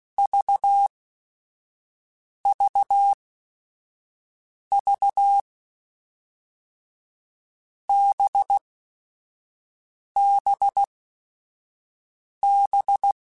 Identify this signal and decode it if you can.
Morse code, heard as a keyed tone, about 16 wpm, spaced as for 5 wpm, 800 Hz. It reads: VVV BBB
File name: Groups@5 / Groups@5 mp3s / VB46@5